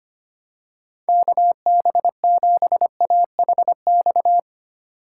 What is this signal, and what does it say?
Morse code, heard as a keyed tone, about 25 wpm, 700 Hz. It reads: KB7AHX
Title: USA Call Signs - 6x Character Spacing - 25wpm